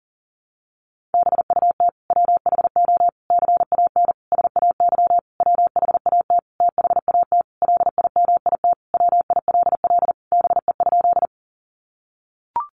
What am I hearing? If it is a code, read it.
Morse code, heard as a keyed tone, about 40 wpm, 700 Hz. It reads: BUT WHO CAN SAY WHAT THAT LIMIT WILL BE?
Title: Sentences from Top 1500 Words 40wpm